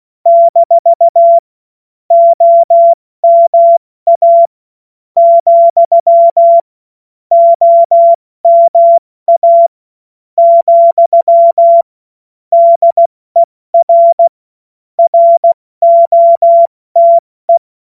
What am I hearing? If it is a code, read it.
Morse code, heard as a keyed tone, about 16 wpm, spaced as for 13 wpm, 680 Hz. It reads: - OMA, OMA, DER ROTE
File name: Morsekurs / Witze / Witz 28. September 2021, 70 F80